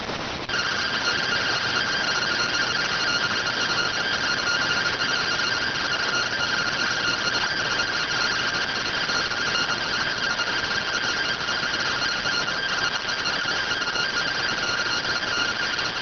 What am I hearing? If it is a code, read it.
Two-tone teleprinter signal, about 170 Hz shift.